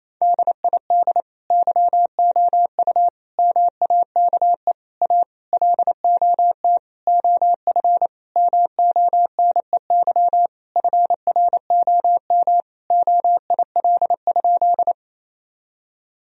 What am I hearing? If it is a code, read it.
Morse code, heard as a keyed tone, about 28 wpm, 700 Hz. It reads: DID YOU MAKE A LOT OF MONEY FROM OIL?
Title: Sentences from Top 1000 Words - Review Part 4 - 28wpm